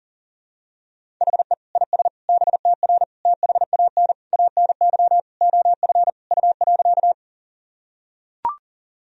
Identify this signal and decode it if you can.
Morse code, heard as a keyed tone, about 40 wpm, 700 Hz. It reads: HE IS BTR THAN ANY OF U.
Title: Sentences from Top 400 Words and Abbreviations 40wpm